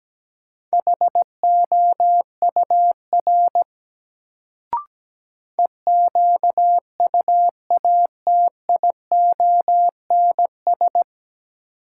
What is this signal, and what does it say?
Morse code, heard as a keyed tone, about 17 wpm, 700 Hz. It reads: HOUR EQUATIONS
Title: Sets of 5 Words - New Word Drill - 17wpm